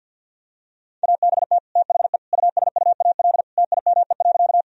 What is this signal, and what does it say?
Morse code, heard as a keyed tone, about 50 wpm, 700 Hz. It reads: ABT THE USUAL TIME.
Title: Sentences from Top 400 Words and Abbreviations 50wpm